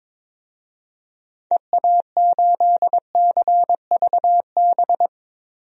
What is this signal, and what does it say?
Morse code, heard as a keyed tone, about 22 wpm, 700 Hz. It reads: EA8CVB
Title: DXCC-100 Call Signs - 6 Characters - 22wpm